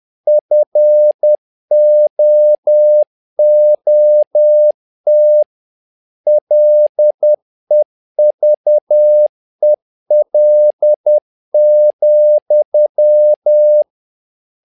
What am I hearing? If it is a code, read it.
Morse code, heard as a keyed tone, about 10 wpm, 600 Hz. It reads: FOOT LEVEL,
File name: learning center / tower trouble / tower trouble 10wpm